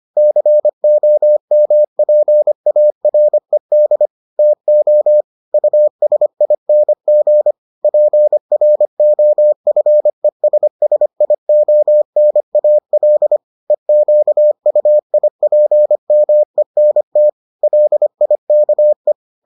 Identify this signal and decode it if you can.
Morse code, heard as a keyed tone, about 25 wpm, 600 Hz. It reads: COMPARED TO USING PROFESSIONAL EQUIPMENT LIKE